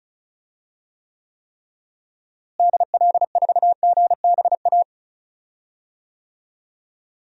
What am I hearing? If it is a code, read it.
Morse code, heard as a keyed tone, about 34 wpm, 700 Hz. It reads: DL4GBA
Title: DXCC-100 Call Signs - 6 Characters - Repeated 3x - 35wpm